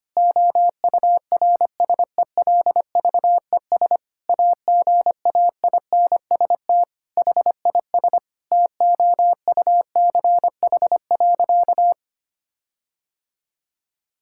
Morse code, heard as a keyed tone, about 25 wpm, 700 Hz. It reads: OURSELVES AGAINST HIS TOUCH.
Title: Sentences from Top 1600 Words 25wpm